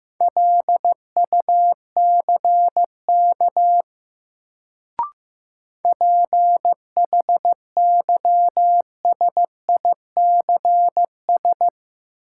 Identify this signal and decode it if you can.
Morse code, heard as a keyed tone, about 15 wpm, 700 Hz. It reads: LUCK PHYSICS
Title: Sentences from Top 1500 Words - New Word Drill Alt - 15wpm